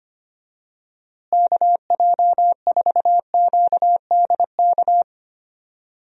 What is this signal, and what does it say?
Morse code, heard as a keyed tone, about 25 wpm, 700 Hz. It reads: KJ4QDK
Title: USA Call Signs - 6 Characters - Repeated 3x - 25wpm